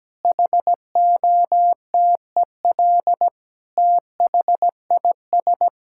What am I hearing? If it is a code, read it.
Morse code, heard as a keyed tone, about 17 wpm, 700 Hz. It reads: HOTEL THIS
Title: Sentences from Top 1600 Words 17wpm